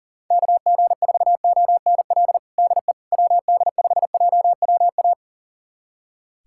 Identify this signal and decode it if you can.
Morse code, heard as a keyed tone, about 40 wpm, 700 Hz. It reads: KG4ONL DE WD5JWA
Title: USA Callsign DE Callsign - 6 Characters - 40wpm